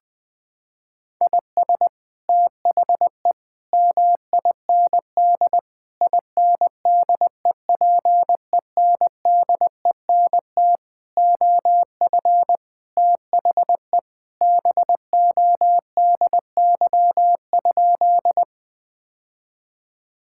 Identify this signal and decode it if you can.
Morse code, heard as a keyed tone, about 21 wpm, 700 Hz. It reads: IS THE MIND INDEPENDENT OF THE BODY?
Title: Sentences from Top 1000 Words - Review Part 6 - 20wpm